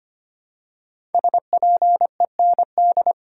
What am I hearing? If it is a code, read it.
Morse code, heard as a keyed tone, about 25 wpm, 700 Hz. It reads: SPEND
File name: 5 Letter Words - First 3 Letters Given - 25wpm